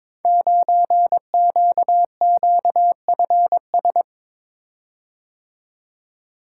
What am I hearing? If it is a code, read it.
Morse code, heard as a keyed tone, about 22 wpm, 700 Hz. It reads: 9QQFS
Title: Code Groups - 5 Letter:Numbers - 22wpm